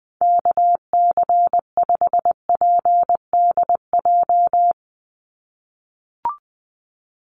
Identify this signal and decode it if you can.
Morse code, heard as a keyed tone, about 20 wpm, 700 Hz. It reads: KC5PDJ
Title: USA Call Signs 20wpm